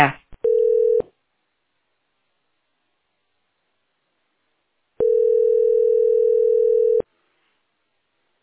Ringback tone (North America), 440 + 480 Hz.